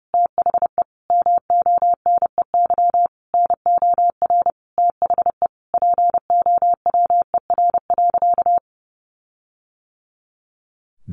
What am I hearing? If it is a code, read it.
Morse code, heard as a keyed tone, about 30 wpm, 700 Hz. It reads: THE MONEY NOR THE POWER.
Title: Sentences from Top 1500 Words 30wpm